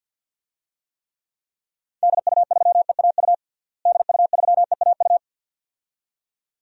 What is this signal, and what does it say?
Morse code, heard as a keyed tone, about 48 wpm, 700 Hz. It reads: DU3EAU DU3EAU
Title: DXCC-100 Call Signs - 6 Characters - Repeated 2x - 50wpm